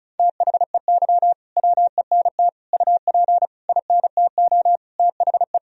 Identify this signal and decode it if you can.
Morse code, heard as a keyed tone, about 36 wpm, 700 Hz. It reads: THEY WENT UP INTO THE
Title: Sentences from Top 1000 Words - Review Part 1 - 35wpm